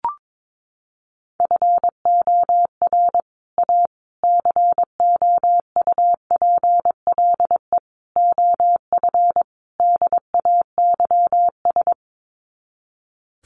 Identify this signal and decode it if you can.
Morse code, heard as a keyed tone, about 22 wpm, 700 Hz. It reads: FOR A COUPLE OF DAYS
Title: Sets of 5 Words Encore2 22wpm